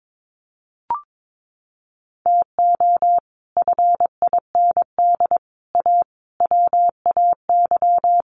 Morse code, heard as a keyed tone, about 22 wpm, 700 Hz. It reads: TO FIND A WAY
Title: Sets of 5 Words 22wpm